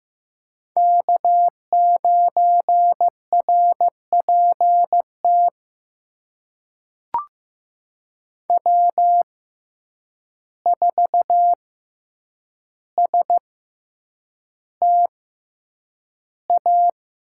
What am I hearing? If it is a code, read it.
Morse code, heard as a keyed tone, about 15 wpm, spaced as for 8 wpm, 700 Hz. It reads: K9RPT W4STA
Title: USA Call Signs - 5 Characters - 6x Character Spacing - 15wpm